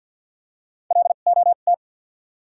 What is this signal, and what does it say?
Morse code, heard as a keyed tone, about 51 wpm, spaced as for 36 wpm, 700 Hz. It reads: LOT
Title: 2x Character Spacing - Top 500 Words - 50wpm